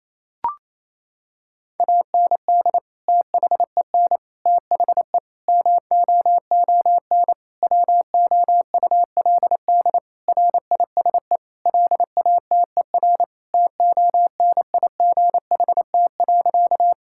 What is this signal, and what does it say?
Morse code, heard as a keyed tone, about 28 wpm, 700 Hz. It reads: AND THEN THE MOON WOULD RISE LATER TONIGHT.
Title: Sentences from Top 1000 Words - Review Part 6 - 28wpm